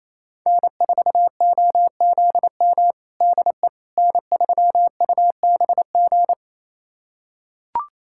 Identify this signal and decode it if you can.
Morse code, heard as a keyed tone, about 28 wpm, 700 Hz. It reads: N4OZM DE N3UBG
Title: USA Callsign DE Callsign - 5 Characters - 28wpm